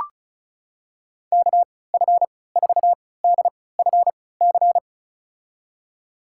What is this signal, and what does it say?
Morse code, heard as a keyed tone, about 36 wpm, spaced as for 20 wpm, 700 Hz. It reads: KF4DFC
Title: USA Call Signs - 6 Characters - 3x Character Spacing - 35wpm